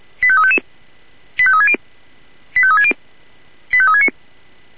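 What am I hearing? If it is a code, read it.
Five-tone selective call (ZVEI), heard as quick runs of five tones, tones of about 70 ms. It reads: 85355 85360 85360 85369
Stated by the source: Five-tone ZVEI selective call received with an SDR.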